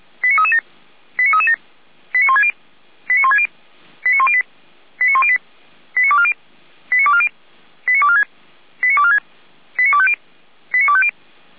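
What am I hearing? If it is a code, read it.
Five-tone selective call (ZVEI), heard as quick runs of five tones, tones of about 70 ms. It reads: 89337 89337 89160 89160 89108 89108 89240 89240 89246 89246 89250 89250
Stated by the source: ZVEI five-tone call, code 89075, received by SDR.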